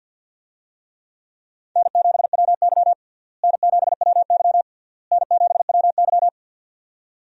Morse code, heard as a keyed tone, about 52 wpm, 700 Hz. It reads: N7WY N7WY N7WY
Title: USA Call Signs - 4 Characters - Repeated 3x - 50wpm